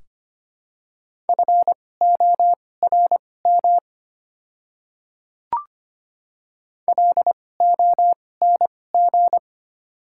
Morse code, heard as a keyed tone, about 25 wpm, spaced as for 18 wpm, 700 Hz. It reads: FORM LONG
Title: 2x Character Spacing - Top 200 Words - 25wpm